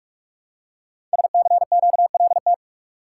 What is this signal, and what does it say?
Morse code, heard as a keyed tone, about 44 wpm, 700 Hz. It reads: SCQLT
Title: Code Groups - 5 Letters - 45wpm